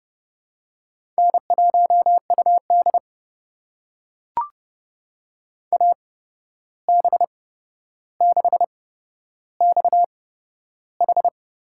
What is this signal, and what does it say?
Morse code, heard as a keyed tone, about 32 wpm, spaced as for 12 wpm, 700 Hz. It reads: N1UD AB6XH
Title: USA Call Signs - 8x Character Spacing - 30wpm